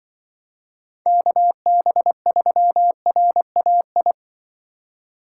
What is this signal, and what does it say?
Morse code, heard as a keyed tone, about 24 wpm, 700 Hz. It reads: KB3RAI
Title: Speed Racing - USA Call Signs 20wpm